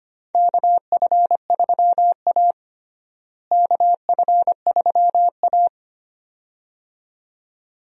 Morse code, heard as a keyed tone, about 25 wpm, 700 Hz. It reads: KF3A KF3A
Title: USA Call Signs - 4 Characters - Repeated 2x - 25wpm